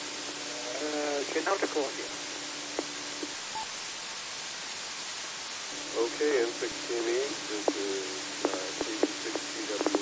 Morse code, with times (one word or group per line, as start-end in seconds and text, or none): none